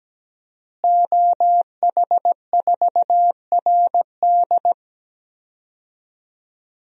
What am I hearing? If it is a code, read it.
Morse code, heard as a keyed tone, about 17 wpm, 700 Hz. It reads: OH4RD
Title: DXCC-100 Call Signs - 5 Characters - Repeated 3x - 17wpm